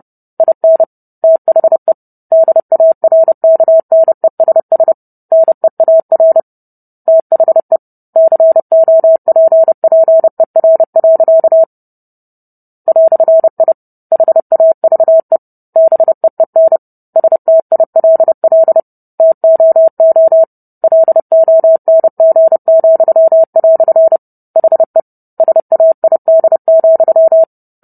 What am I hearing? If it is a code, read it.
Morse code, heard as a keyed tone, about 30 wpm, 650 Hz. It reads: IN THE DARKNESS NEAR THE COPPER. "I HAVE BEEN STILL TOO LONG," HE SAID,